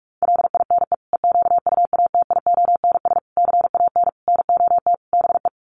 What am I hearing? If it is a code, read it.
Morse code, heard as a keyed tone, about 44 wpm, 700 Hz. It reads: LINE EQUATIONS CAN NOT BE
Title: Sets of 5 Words 45wpm